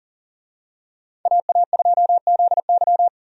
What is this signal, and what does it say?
Morse code, heard as a keyed tone, about 40 wpm, 700 Hz. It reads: AA2ZY
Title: USA Call Signs - 5 Characters - 40wpm